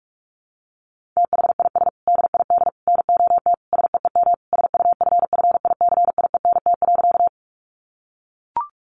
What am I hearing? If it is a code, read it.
Morse code, heard as a keyed tone, about 44 wpm, 700 Hz. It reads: THIS DID NOT SEEM SUFFICIENT.
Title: Sentences from Top 1400 Words 45wpm